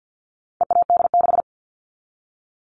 Morse code, heard as a keyed tone, about 50 wpm, 700 Hz. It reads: EAD6
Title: All Cut Numbers - 4-Digit 50wpm